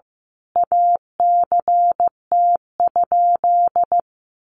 Morse code, heard as a keyed tone, about 15 wpm, 700 Hz. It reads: ACT?
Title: Sentences from Top 1400 Words 15wpm